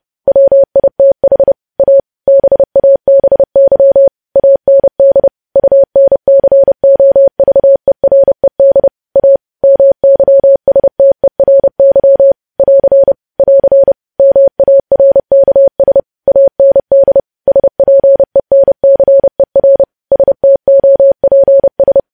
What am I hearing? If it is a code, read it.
Morse code, heard as a keyed tone, about 30 wpm, 550 Hz. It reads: WITH A BABY AND UNCOVERED A MYSTERY <AR> <AR> MARKS AND SPENCER STOPS
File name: podcast / News headlines-1x-30x30